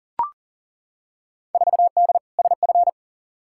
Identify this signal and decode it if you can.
Morse code, heard as a keyed tone, about 40 wpm, 700 Hz. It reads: 4D SF
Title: Field Day Exchanges Encore - 40wpm